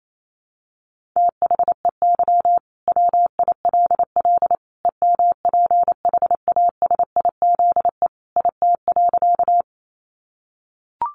Morse code, heard as a keyed tone, about 28 wpm, 700 Hz. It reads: THEY WILL EMPHASIZE IT.
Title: Sentences from Top 1500 Words 28wpm